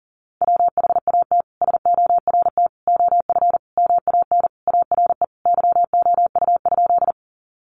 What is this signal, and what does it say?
Morse code, heard as a keyed tone, about 40 wpm, 700 Hz. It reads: WHAT SORT OF MAN ARE YOU?
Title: Sentences from Top 1300 Words 40wpm